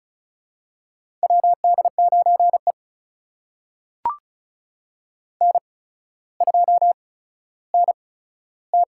Morse code, heard as a keyed tone, about 36 wpm, spaced as for 15 wpm, 700 Hz. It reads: WD9E N2NT
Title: USA Call Signs - 4 Characters - 8x Character Spacing - 35wpm